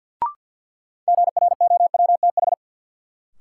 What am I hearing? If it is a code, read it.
Morse code, heard as a keyed tone, about 53 wpm, 700 Hz. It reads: GROWTH
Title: Sentences from Top 1000 Words - Review Part 5 - New Word Drill - 50wpm